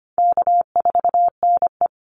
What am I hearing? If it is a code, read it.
Morse code, heard as a keyed tone, about 25 wpm, 700 Hz. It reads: K4NE
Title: Call Signs Repeated 3x - 25wpm